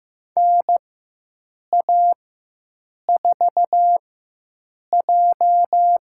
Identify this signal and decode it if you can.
Morse code, heard as a keyed tone, about 15 wpm, spaced as for 7 wpm, 700 Hz. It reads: NA4J
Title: USA Call Signs - 4 Characters - 4x Character Spacing - 15wpm